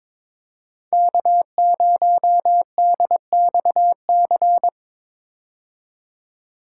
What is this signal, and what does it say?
Morse code, heard as a keyed tone, about 22 wpm, 700 Hz. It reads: K0DXC
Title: K1USN SST QSO Practice - 22wpm